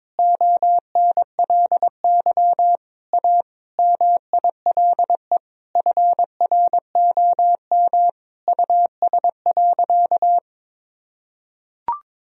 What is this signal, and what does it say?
Morse code, heard as a keyed tone, about 21 wpm, 700 Hz. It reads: ONLY A MILE FROM US.
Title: Sentences from Top 1000 Words - Review Part 1 - 22wpm